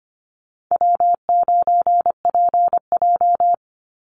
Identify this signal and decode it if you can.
Morse code, heard as a keyed tone, about 25 wpm, 700 Hz. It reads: W9PJ
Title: Easy Call Signs Repeated 3x - 25wpm